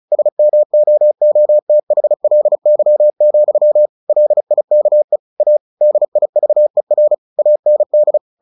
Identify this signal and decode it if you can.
Morse code, heard as a keyed tone, about 34 wpm, 600 Hz. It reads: SMOOTHLY, LIKE A DIVER AND
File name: the old man and the sea / the old man and the sea 1 35wpm